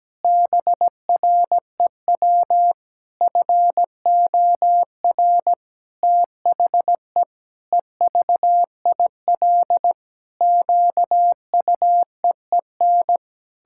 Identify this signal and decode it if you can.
Morse code, heard as a keyed tone, about 17 wpm, 700 Hz. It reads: BREW FOR THE EVIL QUEEN
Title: Pangram Challenge v2 - 17wpm